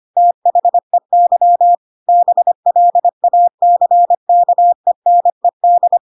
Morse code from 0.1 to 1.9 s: THEY